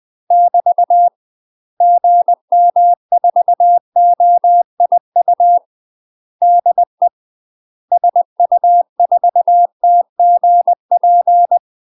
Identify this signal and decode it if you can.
Morse code, heard as a keyed tone, about 20 wpm, 700 Hz. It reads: <BT> GM4OIU DE SU4TGP
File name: UK MASTER QSOs TRAINING / 20WPM / UK MASTER QSOs 0011